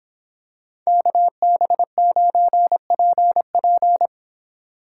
Morse code, heard as a keyed tone, about 26 wpm, 700 Hz. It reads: KB9PP